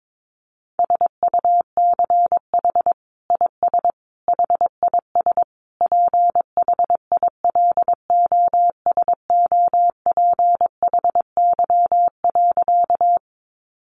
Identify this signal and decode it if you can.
Morse code, heard as a keyed tone, about 22 wpm, 700 Hz. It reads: SUCH IS HIS PHILOSOPHY.
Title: Sentences from Top 900 Words 22wpm